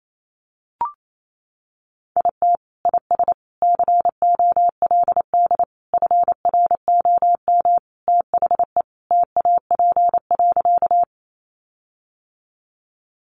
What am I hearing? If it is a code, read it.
Morse code, heard as a keyed tone, about 28 wpm, 700 Hz. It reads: IT IS COLD FROM THE TAP.
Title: Sentences from Top 1400 Words 28wpm